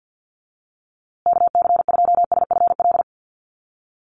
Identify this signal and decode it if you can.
Morse code, heard as a keyed tone, about 50 wpm, 700 Hz. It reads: KC2SFL
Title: USA Call Signs Repeated 2x 50wpm